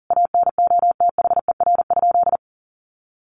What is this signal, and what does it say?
Morse code, heard as a keyed tone, about 40 wpm, 700 Hz. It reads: ANOTHER?
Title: Sentences from Top 1400 Words 40wpm